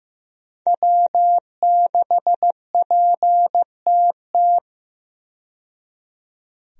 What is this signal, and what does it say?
Morse code, heard as a keyed tone, about 15 wpm, 700 Hz. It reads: W6PTT
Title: USA Call Signs - 5 Characters - 15wpm